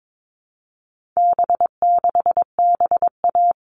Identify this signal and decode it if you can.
Morse code, heard as a keyed tone, about 22 wpm, 700 Hz. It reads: B6BA
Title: All Cut Numbers - 4-Digit 22wpm